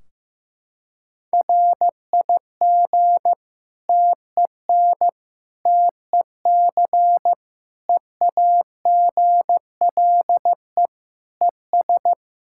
Morse code, heard as a keyed tone, about 16 wpm, 700 Hz. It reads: RIG TEN TEC EAGLE ES
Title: Practice QSO - Exchange 2 - Info - 15wpm